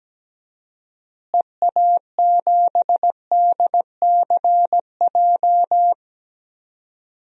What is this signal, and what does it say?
Morse code, heard as a keyed tone, about 17 wpm, 700 Hz. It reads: EA7DCJ
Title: DXCC-100 Call Signs - 6 Characters - Repeated 2x - 17wpm